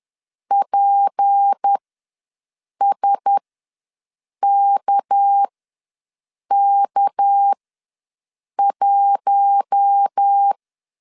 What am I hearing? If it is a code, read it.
Morse code, heard as a keyed tone, about 10 wpm, spaced as for 6 wpm, 800 Hz. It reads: PSKK1
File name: LEZIONI / 6 PKX0 / 6 PK ws 10wpm 3745